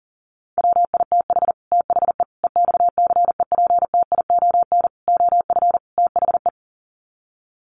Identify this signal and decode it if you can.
Morse code, heard as a keyed tone, about 37 wpm, 700 Hz. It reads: WITH THE EXCEPTION OF THE
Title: Sets of 5 Words Encore 40wpm